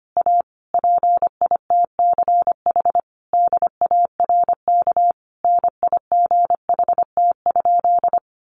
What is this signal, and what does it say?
Morse code, heard as a keyed tone, about 25 wpm, 700 Hz. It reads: A PITCH DARK NIGHT?
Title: Sentences from Top 1300 Words 25wpm